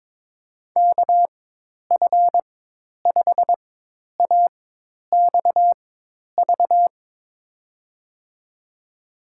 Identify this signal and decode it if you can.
Morse code, heard as a keyed tone, about 22 wpm, spaced as for 10 wpm, 700 Hz. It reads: KF5AXV
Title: USA Call Signs - 6 Characters - 4x Character Spacing - 22wpm